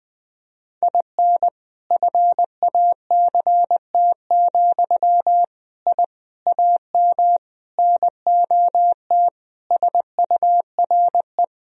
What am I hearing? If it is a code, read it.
Morse code, heard as a keyed tone, about 20 wpm, 700 Hz. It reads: IN FACT, I AM NOT SURE
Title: Sentences from Top 1000 Words - Review Part 3 - 20wpm